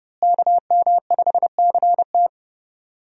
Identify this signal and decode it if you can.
Morse code, heard as a keyed tone, about 32 wpm, 700 Hz. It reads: KM5CT
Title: USA Call Signs - 5 Characters - Repeated 2x - 30wpm